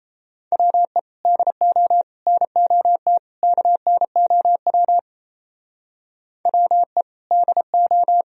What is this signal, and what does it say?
Morse code, heard as a keyed tone, about 34 wpm, 700 Hz. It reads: WE DO NOT KNOW WE DO
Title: Speed-Racing - Sets of 4 Words 28wpm